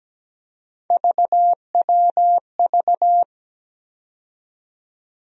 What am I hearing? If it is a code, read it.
Morse code, heard as a keyed tone, about 17 wpm, 700 Hz. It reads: VWV
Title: USA Call Sign Postfixes - 3 Letters 17wpm